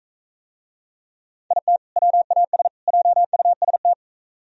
Morse code, heard as a keyed tone, about 45 wpm, 700 Hz. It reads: IT WAS JUST